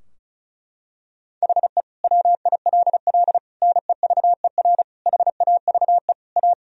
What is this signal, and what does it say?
Morse code, heard as a keyed tone, about 34 wpm, 700 Hz. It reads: HE WILL NEVER HAVE A
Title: Sentences from Top 1000 Words - Review Part 2 - 35wpm